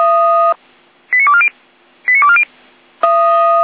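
Five-tone selective call (ZVEI), heard as quick runs of five tones, tones of about 70 ms. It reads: 89240 89240
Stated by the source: ZVEI five-tone selective call received on an SDR.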